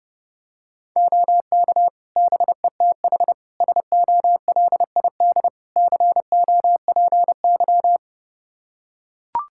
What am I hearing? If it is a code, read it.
Morse code, heard as a keyed tone, about 32 wpm, 700 Hz. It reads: OK BETH SOLID COPY
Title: Practice QSO - Exchange 3 - More Info - 30wpm